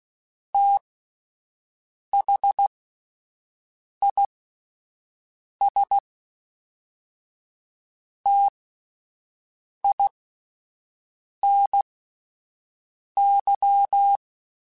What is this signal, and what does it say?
Morse code, heard as a keyed tone, about 16 wpm, spaced as for 5 wpm, 800 Hz. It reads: THIS TINY
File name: T Runs / T Runs mp3s / T Runs @5